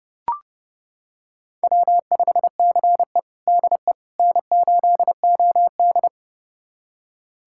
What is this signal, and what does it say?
Morse code, heard as a keyed tone, about 30 wpm, 700 Hz. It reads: W5CE DE N8OD
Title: USA Callsign DE Callsign - 4 Characters - 30wpm